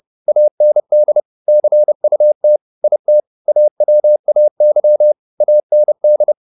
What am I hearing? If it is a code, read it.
Morse code, heard as a keyed tone, about 30 wpm, 600 Hz. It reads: AND CUT IT AWAY AND
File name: the old man and the sea / the old man and the sea 1 30wpm